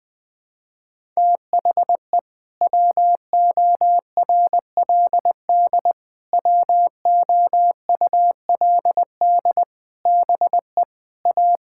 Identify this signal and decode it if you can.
Morse code, heard as a keyed tone, about 20 wpm, 700 Hz. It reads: THE WORLD WOULD BE A
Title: Sentences with 8-Word Phrases - 20wpm